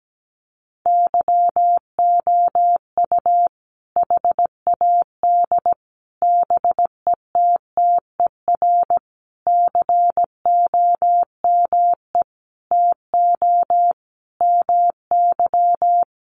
Morse code, heard as a keyed tone, about 17 wpm, 700 Hz. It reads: YOU HAD BETTER COME TO MY
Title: Sentences from Top 400 Words 17wpm